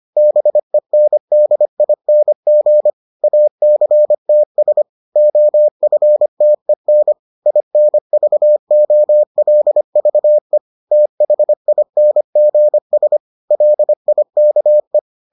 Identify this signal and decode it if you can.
Morse code, heard as a keyed tone, about 25 wpm, 600 Hz. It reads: BENDING ACTS OFTEN INVOLVE THINGS LIKE